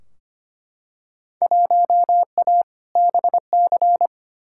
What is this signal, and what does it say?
Morse code, heard as a keyed tone, about 25 wpm, 700 Hz. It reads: 1A BC
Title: Field Day Exchanges - 25wpm